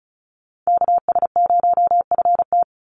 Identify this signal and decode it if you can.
Morse code, heard as a keyed tone, about 34 wpm, 700 Hz. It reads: KS0FT
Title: USA Call Signs 35wpm